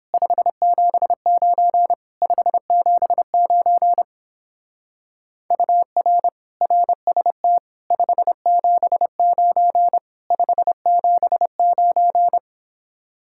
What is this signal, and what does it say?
Morse code, heard as a keyed tone, about 31 wpm, 700 Hz. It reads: 579 579 UR RST 579 579